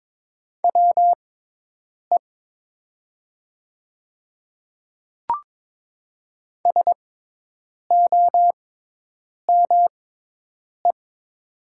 Morse code, heard as a keyed tone, about 22 wpm, spaced as for 8 wpm, 700 Hz. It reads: WE SOME
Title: 6x Character Spacing - Top 100 Words - 22wpm